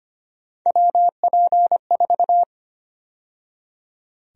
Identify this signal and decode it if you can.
Morse code, heard as a keyed tone, about 25 wpm, 700 Hz. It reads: WP4
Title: USA Call Sign Prefixes - 25wpm